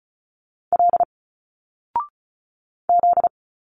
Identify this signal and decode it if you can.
Morse code, heard as a keyed tone, about 36 wpm, 700 Hz. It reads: L Z
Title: Single Letters 35wpm